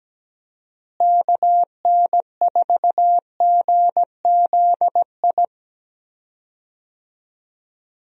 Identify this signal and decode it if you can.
Morse code, heard as a keyed tone, about 17 wpm, 700 Hz. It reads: KN4GZI
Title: USA Call Signs - 6 Characters - Repeated 3x - 17wpm